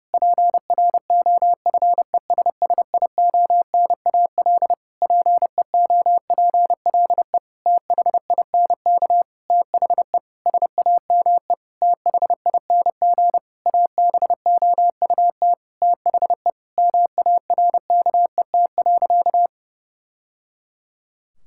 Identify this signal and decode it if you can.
Morse code, heard as a keyed tone, about 30 wpm, 700 Hz. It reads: PROFESSIONAL PEOPLE THINK THE SAME THING ABOUT THE MARKET.
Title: Sentences from Top 1000 Words - Review Part 5 - 30wpm